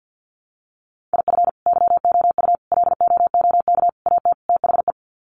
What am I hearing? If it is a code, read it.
Morse code, heard as a keyed tone, about 48 wpm, 700 Hz. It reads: IF YOU LOOK AT THE
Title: Sets of 5 Words 50wpm